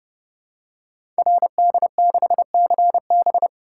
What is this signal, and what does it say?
Morse code, heard as a keyed tone, about 30 wpm, 700 Hz. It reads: RD6CB
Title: Code Groups - 5 Letter:Numbers - 30wpm